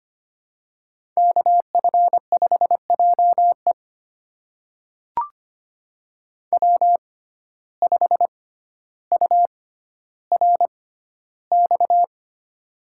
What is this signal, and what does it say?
Morse code, heard as a keyed tone, about 25 wpm, spaced as for 13 wpm, 700 Hz. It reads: KF5JE W5URX
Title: USA Call Signs - 5 Characters - 6x Character Spacing - 25wpm